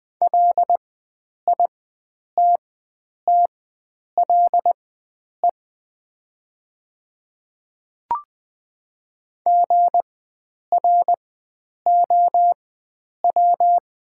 Morse code, heard as a keyed tone, about 20 wpm, spaced as for 9 wpm, 700 Hz. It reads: LITTLE GROW
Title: 4x Character Spacing - Top 300 Words - 20wpm